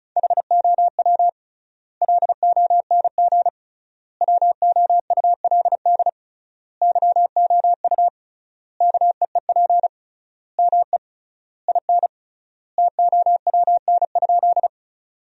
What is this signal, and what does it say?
Morse code, heard as a keyed tone, about 36 wpm, 700 Hz. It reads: HOW LONG WOULD YOU KEEP ME IN TOWN?
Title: 3x Word-Spacing - Sentences from Top 500 Words - 35wpm